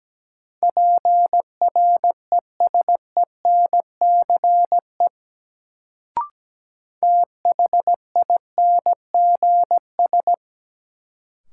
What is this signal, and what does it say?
Morse code, heard as a keyed tone, about 17 wpm, 700 Hz. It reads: PRESENCE THINGS
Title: Sets of 3 Words - New Word Drill - 17wpm